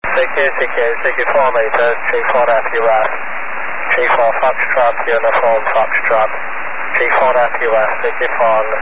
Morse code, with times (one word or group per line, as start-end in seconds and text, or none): none